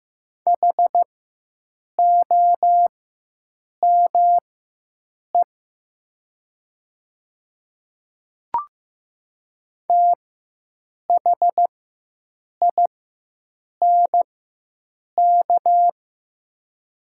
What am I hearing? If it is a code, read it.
Morse code, heard as a keyed tone, about 15 wpm, spaced as for 7 wpm, 700 Hz. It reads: HOME THINK
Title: 4x Character Spacing - Top 200 Words - 15wpm